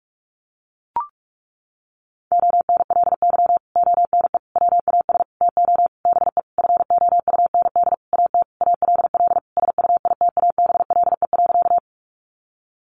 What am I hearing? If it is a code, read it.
Morse code, heard as a keyed tone, about 44 wpm, 700 Hz. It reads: ONLY ONE WAS TO BE FOUND AT ALL SUITABLE.
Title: Sentences from Top 1400 Words 45wpm